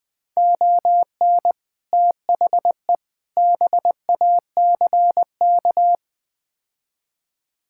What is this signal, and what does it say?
Morse code, heard as a keyed tone, about 20 wpm, 700 Hz. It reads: ON THE BACK
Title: Build A Phrase - 4-Word Phrases - 20wpm